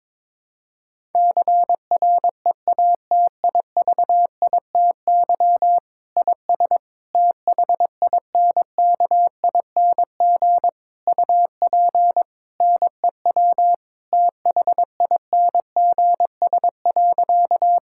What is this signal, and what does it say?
Morse code, heard as a keyed tone, about 22 wpm, 700 Hz. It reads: CREATIVITY IS THINKING UP NEW THINGS.
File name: Difficult Sentences with Hints - 22wpm